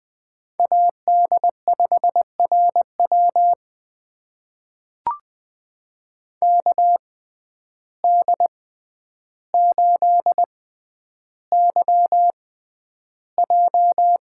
Morse code, heard as a keyed tone, about 20 wpm, spaced as for 10 wpm, 700 Hz. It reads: AD5RW KD8YJ
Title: USA Call Signs - 6x Character Spacing - 20wpm